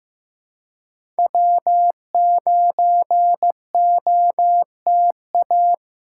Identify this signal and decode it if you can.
Morse code, heard as a keyed tone, about 15 wpm, 700 Hz. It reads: W9OTA
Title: USA Call Signs - 5 Characters - 3x Character Spacing - 15wpm